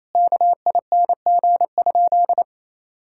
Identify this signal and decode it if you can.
Morse code, heard as a keyed tone, about 28 wpm, 700 Hz. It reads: KING?
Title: Sentences from Top 1000 Words - Review Part 6 - 28wpm